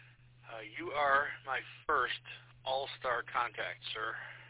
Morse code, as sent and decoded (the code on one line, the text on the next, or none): none